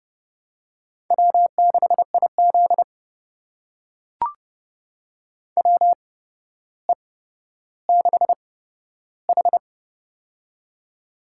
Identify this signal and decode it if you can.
Morse code, heard as a keyed tone, about 32 wpm, spaced as for 13 wpm, 700 Hz. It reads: W6IZ WE6H
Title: USA Call Signs - 4 Characters - 8x Character Spacing - 30wpm